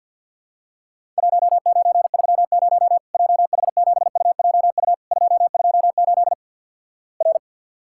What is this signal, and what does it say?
Morse code, heard as a keyed tone, about 50 wpm, 700 Hz. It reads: 1930 JH7UJU 228 R
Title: Worked All Europe DX Contest - QTC Exchange with Repeats - 50wpm